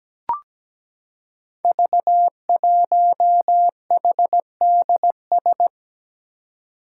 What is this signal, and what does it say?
Morse code, heard as a keyed tone, about 17 wpm, 700 Hz. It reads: V1HDS
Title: Five Challenging Letter-Number Combinations 17wpm